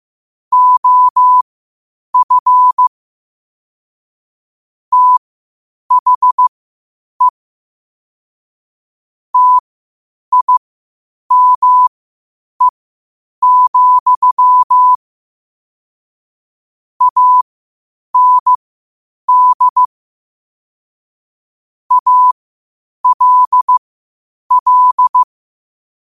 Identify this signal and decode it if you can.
Morse code, heard as a keyed tone, about 15 wpm, spaced as for 8 wpm, 1000 Hz. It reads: OF THE TIME, AND ALL